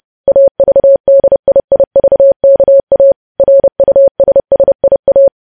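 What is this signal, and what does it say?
Morse code, heard as a keyed tone, about 30 wpm, 550 Hz. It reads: AVDIIVKA RUSSIA
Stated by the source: Morse code news headlines sent at 30 words per minute.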